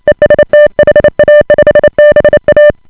Morse code, heard as a keyed tone, about 28 wpm, 590 Hz. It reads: EST HA5BA